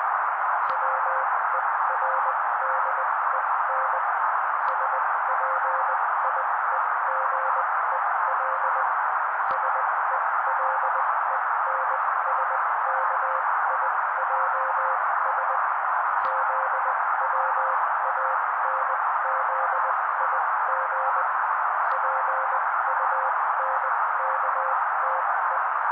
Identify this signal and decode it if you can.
Morse code, heard as a keyed tone, about 20 wpm, spaced as for 16 wpm, 530 Hz. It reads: WERDEN SPIEGEL SELENSKIJS ZWANZIG PUNKTE
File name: daily / 16wpm noise